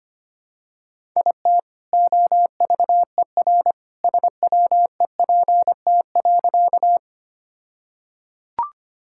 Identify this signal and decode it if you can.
Morse code, heard as a keyed tone, about 25 wpm, 700 Hz. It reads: IT OVER SWEPT.
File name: practice / 25wpm / All-Age Appropriate Jokes - 25wpm